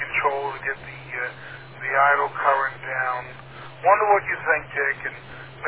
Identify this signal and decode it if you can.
Emergency Alert System (SAME) tones, heard as harsh buzzing bursts, with speech. It is heard from 1.1 to 4.6 s.